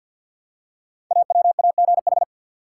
Morse code, heard as a keyed tone, about 47 wpm, 700 Hz. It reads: AWAGH
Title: Code Groups - 5 Letter:Numbers - 50wpm